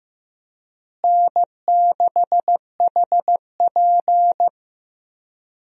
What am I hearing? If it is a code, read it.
Morse code, heard as a keyed tone, about 15 wpm, 700 Hz. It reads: N6HP
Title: USA Call Signs - 4 Characters - Repeated 2x - 15wpm